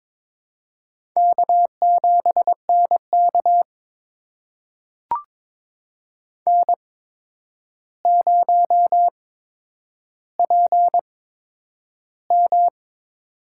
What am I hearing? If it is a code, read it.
Morse code, heard as a keyed tone, about 22 wpm, spaced as for 9 wpm, 700 Hz. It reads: K7NK N0PM
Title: USA Call Signs - 4 Characters - 8x Character Spacing - 22wpm